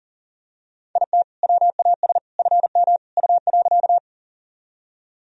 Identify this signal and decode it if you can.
Morse code, heard as a keyed tone, about 40 wpm, 700 Hz. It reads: IT WAS FM U.
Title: Sentences from Top 100 Words and Abbreviations 40wpm